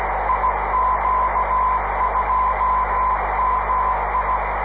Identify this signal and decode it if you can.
Morse code, heard as a keyed tone, about 16 wpm, 1020 Hz, with radio noise. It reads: IQ2MI
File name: 500khz / 30 december 2011 / iq2mi qso-1